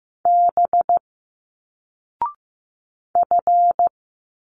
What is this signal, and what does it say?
Morse code, heard as a keyed tone, about 15 wpm, 700 Hz. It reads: B F
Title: Single Letters 15wpm